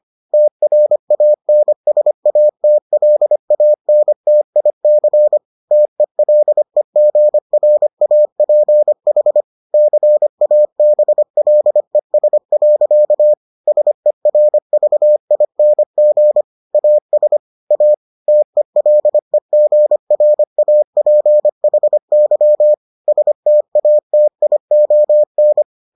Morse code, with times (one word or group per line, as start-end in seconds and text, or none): 0.3-5.4 TRANSATLANTIC
5.7-9.4 TELEGRAPH
9.7-13.3 CABLES.
13.7-16.4 SERVING
16.7-17.4 AS
17.7-17.9 A
18.3-22.7 TELEGRAPHY
23.1-25.6 STATION